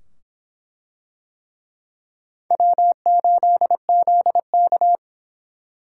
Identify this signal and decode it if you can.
Morse code, heard as a keyed tone, about 24 wpm, 700 Hz. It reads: W8ZK